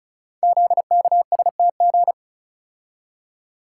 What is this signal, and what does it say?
Morse code, heard as a keyed tone, about 36 wpm, 700 Hz. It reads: ZKSTG
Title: Code Groups - 5 Letters - 35wpm